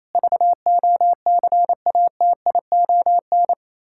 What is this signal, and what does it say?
Morse code, heard as a keyed tone, about 28 wpm, 700 Hz. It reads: VOCATION
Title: Pangram Challenge v2 - 28wpm